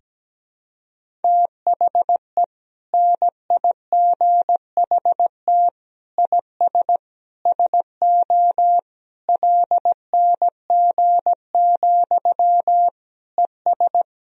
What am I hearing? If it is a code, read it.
Morse code, heard as a keyed tone, about 17 wpm, 700 Hz. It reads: THE NIGHT IS SO LNG, ES
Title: Sentences from Top 500 Words and Abbreviations 17wpm